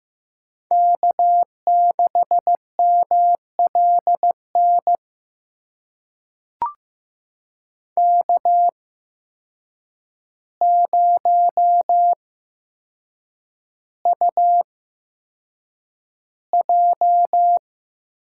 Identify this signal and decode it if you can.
Morse code, heard as a keyed tone, about 15 wpm, spaced as for 7 wpm, 700 Hz. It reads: K6MLN K0UJ
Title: USA Call Signs - 8x Character Spacing - 15wpm